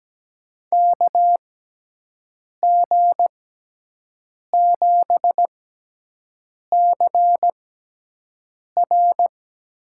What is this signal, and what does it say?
Morse code, heard as a keyed tone, about 17 wpm, spaced as for 6 wpm, 700 Hz. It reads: KG7CR
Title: USA Call Signs - 6x Character Spacing - 17wpm